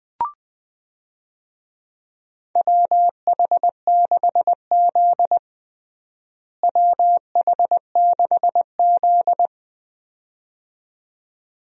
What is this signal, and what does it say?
Morse code, heard as a keyed tone, about 20 wpm, 700 Hz. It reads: WH6Z WH6Z
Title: USA Call Signs - 4 Characters - Repeated 2x - 20wpm